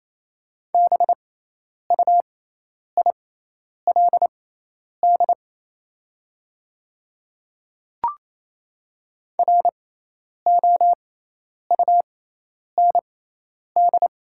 Morse code, heard as a keyed tone, about 28 wpm, spaced as for 10 wpm, 700 Hz. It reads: BUILD ROUND
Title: 6x Character Spacing - Top 300 Words - 28wpm